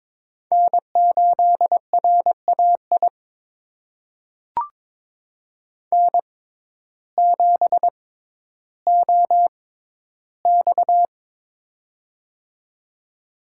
Morse code, heard as a keyed tone, about 22 wpm, spaced as for 12 wpm, 700 Hz. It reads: N8RAI N7OX
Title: USA Call Signs - 6x Character Spacing - 22wpm